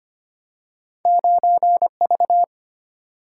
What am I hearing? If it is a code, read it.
Morse code, heard as a keyed tone, about 25 wpm, 700 Hz. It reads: 9V
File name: Two Letter-Number Combinations - 25wpm